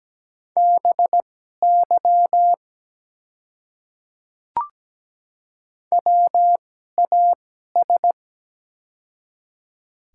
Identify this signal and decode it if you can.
Morse code, heard as a keyed tone, about 17 wpm, spaced as for 12 wpm, 690 Hz. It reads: BY WAS
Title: 2x Character Spacing - Top 100 Words - 17wpm